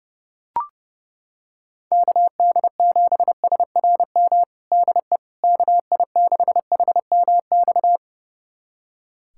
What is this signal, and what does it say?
Morse code, heard as a keyed tone, about 30 wpm, 700 Hz. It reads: KD7SRM DE KI6HMX
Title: USA Callsign DE Callsign 30wpm